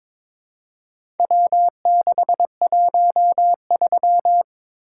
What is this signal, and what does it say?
Morse code, heard as a keyed tone, about 22 wpm, 700 Hz. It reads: W613
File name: practice / 22wpm / Four Letter-Number Combinations - 22wpm